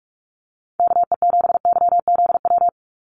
Morse code, heard as a keyed tone, about 44 wpm, 700 Hz. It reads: KE7YZW
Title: USA Call Signs 45wpm